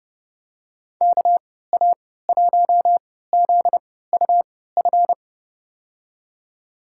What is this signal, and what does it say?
Morse code, heard as a keyed tone, about 32 wpm, spaced as for 17 wpm, 700 Hz. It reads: KA1ZUF
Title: USA Call Signs - 6 Characters - 3x Character Spacing - 30wpm